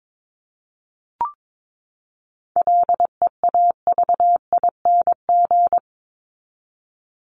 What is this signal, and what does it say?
Morse code, heard as a keyed tone, about 22 wpm, 700 Hz. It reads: LEAVING
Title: ING Suffix - Top 100 Words with Base Words 22wpm